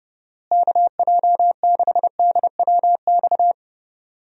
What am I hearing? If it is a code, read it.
Morse code, heard as a keyed tone, about 32 wpm, 700 Hz. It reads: KJ6DWX
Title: USA Call Signs - 6 Characters - Repeated 3x - 30wpm